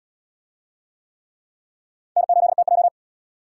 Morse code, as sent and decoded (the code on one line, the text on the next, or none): -. ---.. - .---
N8TJ